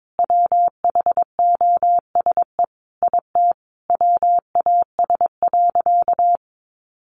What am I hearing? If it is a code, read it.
Morse code, heard as a keyed tone, about 22 wpm, 700 Hz. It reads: WHOSE IT WAS.
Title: Sentences from Top 1200 Words 22wpm